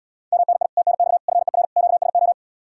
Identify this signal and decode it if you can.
Morse code, heard as a keyed tone, about 77 wpm, 700 Hz. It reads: MNI TNX FER QSO